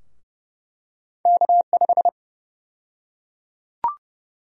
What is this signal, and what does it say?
Morse code, heard as a keyed tone, about 30 wpm, 700 Hz. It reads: K5
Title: USA Call Sign Prefixes - Repeated 2x - 30wpm